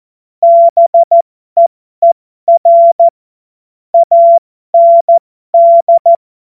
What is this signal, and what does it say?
Morse code, heard as a keyed tone, about 14 wpm, spaced as for 12 wpm, 680 Hz. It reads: BEER AND
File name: Morsekurs / Witze